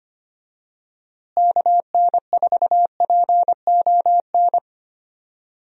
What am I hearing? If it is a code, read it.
Morse code, heard as a keyed tone, about 25 wpm, 700 Hz. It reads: KN4PON
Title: USA Call Signs - 6 Characters - Repeated 3x - 25wpm